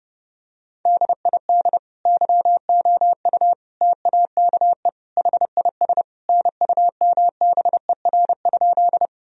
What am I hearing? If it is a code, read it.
Morse code, heard as a keyed tone, about 29 wpm, 700 Hz. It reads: DID YOU TAKE HIS NUMBER?
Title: Sentences from Top 1000 Words - Review Part 1 - 30wpm